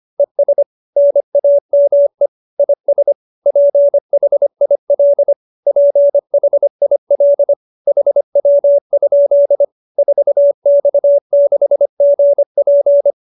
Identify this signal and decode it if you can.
Morse code, heard as a keyed tone, about 25 wpm, 570 Hz. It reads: ES NAME IS PHIL PHIL HW? 4X6GP
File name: QSO315 25